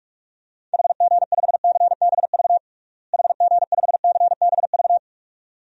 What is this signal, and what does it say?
Morse code, heard as a keyed tone, about 41 wpm, 700 Hz. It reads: HG5CBV HG5CBV
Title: DXCC-100 Call Signs - 6 Characters - Repeated 2x - 45wpm